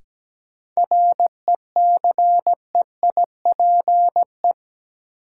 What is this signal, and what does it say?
Morse code, heard as a keyed tone, about 17 wpm, 700 Hz. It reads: RECEIPE
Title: Sentences from Top 1200 Words - New Word Drill - 17wpm